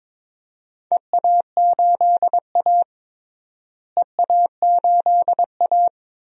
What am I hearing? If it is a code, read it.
Morse code, heard as a keyed tone, about 22 wpm, 700 Hz. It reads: EA8A EA8A
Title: DXCC-100 Call Signs - 4 Characters - Repeated 3x - 22wpm